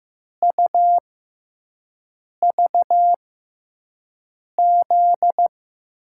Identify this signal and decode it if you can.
Morse code, heard as a keyed tone, about 15 wpm, spaced as for 5 wpm, 700 Hz. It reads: UVZ
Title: USA Call Signs - 5 Characters - 6x Character Spacing - 15wpm